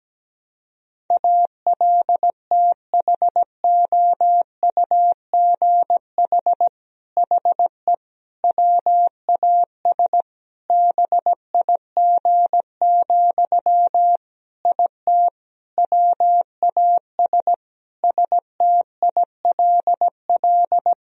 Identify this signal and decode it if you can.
Morse code, heard as a keyed tone, about 17 wpm, 700 Hz. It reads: ALTHOUGH HE WAS BIG, IT WAS STILL
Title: Sentences from Top 1000 Words - Review Part 3 - 17wpm